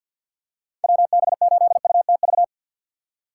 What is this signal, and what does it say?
Morse code, heard as a keyed tone, about 51 wpm, 700 Hz. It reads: WB8UTV
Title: USA Call Signs - 6 Characters - 6x Character Spacing - 50wpm